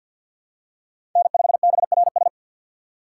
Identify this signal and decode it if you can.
Morse code, heard as a keyed tone, about 53 wpm, 700 Hz. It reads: N5BRS
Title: USA Call Signs - 5 Characters - 3x Character Spacing - 50wpm